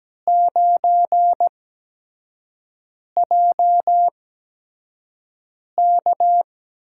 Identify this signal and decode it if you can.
Morse code, heard as a keyed tone, about 17 wpm, spaced as for 5 wpm, 700 Hz. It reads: 9JK
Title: USA Call Signs - 8x Character Spacing - 17wpm